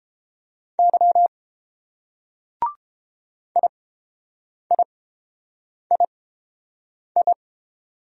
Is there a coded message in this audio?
Yes: Morse code from 0.8 to 7.3 s, heard as a keyed tone, about 30 wpm, spaced as for 7 wpm, 700 Hz. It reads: Y IIII